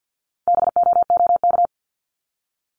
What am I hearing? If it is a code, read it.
Morse code, heard as a keyed tone, about 50 wpm, 690 Hz. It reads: BOOK
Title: Multi-letter Phonemes Repeated 3x 50wpm